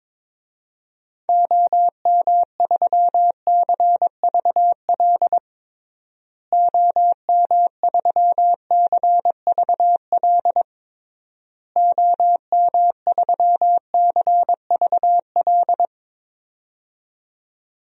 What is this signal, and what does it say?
Morse code, heard as a keyed tone, about 21 wpm, 700 Hz. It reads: OM3CVL OM3CVL OM3CVL
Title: DXCC-100 Call Signs - 6 Characters - Repeated 3x - 22wpm